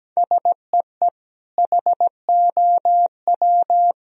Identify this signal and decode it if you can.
Morse code, heard as a keyed tone, about 17 wpm, 700 Hz. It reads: SEE HOW
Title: Sentences from Top 1000 Words - Review Part 2 - 17wpm